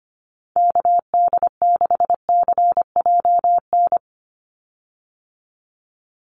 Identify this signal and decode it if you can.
Morse code, heard as a keyed tone, about 25 wpm, 690 Hz. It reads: KD6CJN
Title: USA Call Signs 25wpm